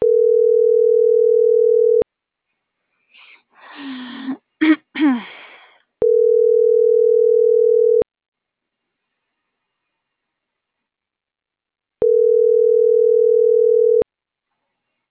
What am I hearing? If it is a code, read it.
Ringback tone (North America), 440 + 480 Hz.